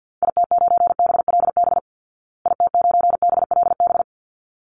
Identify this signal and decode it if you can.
Morse code, heard as a keyed tone, about 48 wpm, 700 Hz. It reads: IT9BLB IT9BLB
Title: Call Signs Repeated 3x - 50wpm